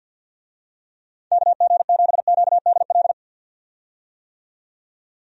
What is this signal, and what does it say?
Morse code, heard as a keyed tone, about 51 wpm, 700 Hz. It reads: KG7QDL
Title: USA Call Signs - 6 Characters - 50wpm